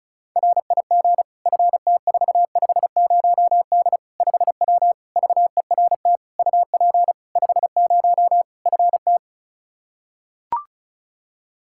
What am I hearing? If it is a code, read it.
Morse code, heard as a keyed tone, about 36 wpm, 700 Hz. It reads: RIG FT450D 5W VERT UP 50 FT
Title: Practice QSO - Exchange 2 - Info - 35wpm